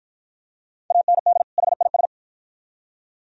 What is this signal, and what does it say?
Morse code, heard as a keyed tone, about 48 wpm, 690 Hz. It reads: AND HIS